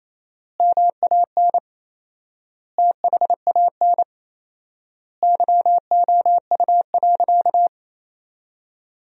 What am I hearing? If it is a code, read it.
Morse code, heard as a keyed tone, about 28 wpm, 700 Hz. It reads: MAN THAN YOU.
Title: 4x Word-Spacing - Sentences from Top 400 Words - 28wpm